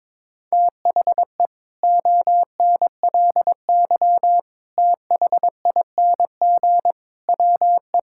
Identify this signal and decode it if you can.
Morse code, heard as a keyed tone, about 22 wpm, 700 Hz. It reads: THE ONLY THING WE
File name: Sets of 9 Words - 22wpm